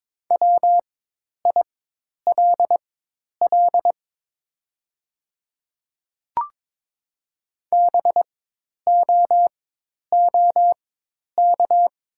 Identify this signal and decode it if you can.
Morse code, heard as a keyed tone, about 22 wpm, spaced as for 10 wpm, 700 Hz. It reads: WILL BOOK
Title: 4x Character Spacing - Top 300 Words - 22wpm